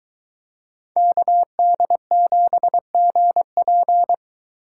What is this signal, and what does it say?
Morse code, heard as a keyed tone, about 22 wpm, 700 Hz. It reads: KD7GP